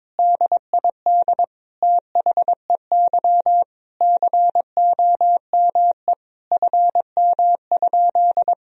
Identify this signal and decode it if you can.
Morse code, heard as a keyed tone, about 22 wpm, 700 Hz. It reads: DID THEY COME FM?
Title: Sentences from Top 200 Words and Abbreviations 22wpm